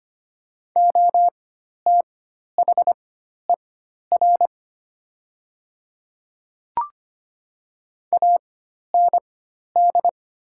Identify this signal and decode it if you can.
Morse code, heard as a keyed tone, about 25 wpm, spaced as for 12 wpm, 700 Hz. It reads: OTHER AND
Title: 4x Character Spacing - Top 100 Words - 25wpm